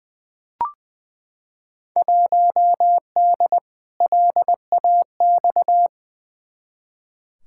Morse code, heard as a keyed tone, about 20 wpm, 700 Hz. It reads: 1D LAX
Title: Field Day Exchanges - 20wpm